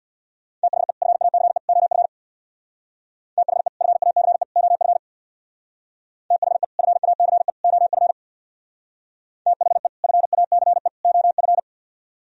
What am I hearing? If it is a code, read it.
Morse code, heard as a keyed tone, about 64 wpm, 700 Hz. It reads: THE FACE OF THE FACE OF THE FACE OF THE FACE OF